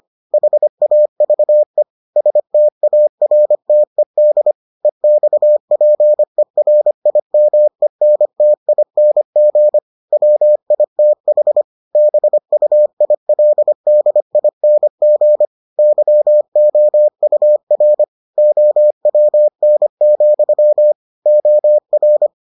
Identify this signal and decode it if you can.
Morse code, heard as a keyed tone, about 25 wpm, 600 Hz. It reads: HAVE STARTED EXPERIMENTING WITH BUILDING YOUR OWN, OR